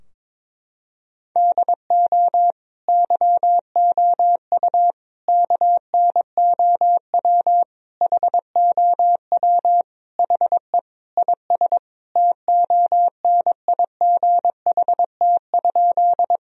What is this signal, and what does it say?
Morse code, heard as a keyed tone, about 22 wpm, 700 Hz. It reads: DO YOU KNOW HOW HE IS TONIGHT?
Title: Sentences from Top 1000 Words - Review Part 3 - 22wpm